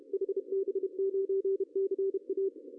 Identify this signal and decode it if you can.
Morse code, heard as a keyed tone, about 32 wpm, 380 Hz. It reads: HB9CA
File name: CW / cqww-cw-k5tr